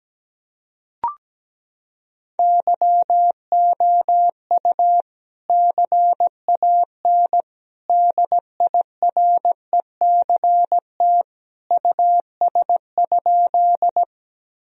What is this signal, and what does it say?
Morse code, heard as a keyed tone, about 17 wpm, 700 Hz. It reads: YOU CAN DIRECT US?
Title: Sentences from Top 1000 Words - Review Part 2 - 17wpm